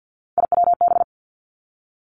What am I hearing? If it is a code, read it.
Morse code, heard as a keyed tone, about 50 wpm, 700 Hz. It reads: IWB
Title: Three Challenging Letter-Number Combinations 50wpm